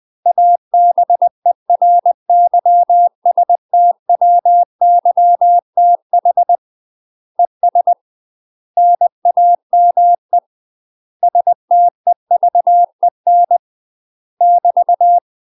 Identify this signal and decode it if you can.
Morse code, heard as a keyed tone, about 20 wpm, 700 Hz. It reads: ABERYSTWYTH ES NAME STEVEN <BT>
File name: UK MASTER QSOs TRAINING / 20WPM / UK MASTER QSOs 0017